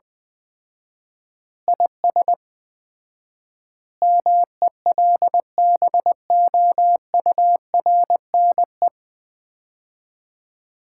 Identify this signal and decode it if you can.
Morse code, heard as a keyed tone, about 20 wpm, 700 Hz. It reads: IS MELBOURNE
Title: Common QSO Phrases - 4x Word-Spacing - 20wpm